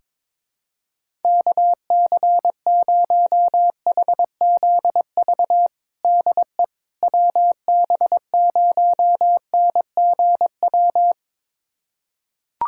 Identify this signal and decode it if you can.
Morse code, heard as a keyed tone, about 22 wpm, 700 Hz. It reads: KC0HZV DE WB0NGW
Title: USA Callsign DE Callsign - 6 Characters - 22wpm